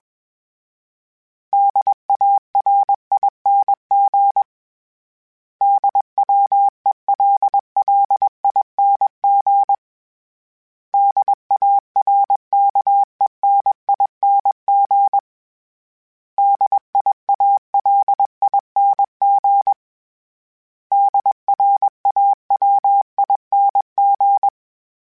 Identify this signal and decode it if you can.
Morse code, heard as a keyed tone, about 21 wpm, 800 Hz. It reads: DARING DWELLING DARKENING DIALING DRAWING